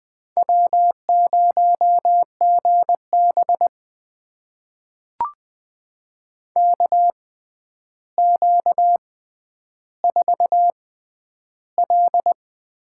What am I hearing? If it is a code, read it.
Morse code, heard as a keyed tone, about 20 wpm, spaced as for 10 wpm, 700 Hz. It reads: W0GB KQ4L
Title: USA Call Signs - 6x Character Spacing - 20wpm